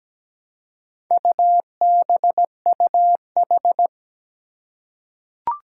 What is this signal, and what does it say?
Morse code, heard as a keyed tone, about 17 wpm, 700 Hz. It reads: UBUH
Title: Four Challenging Letter-Number Combinations 17wpm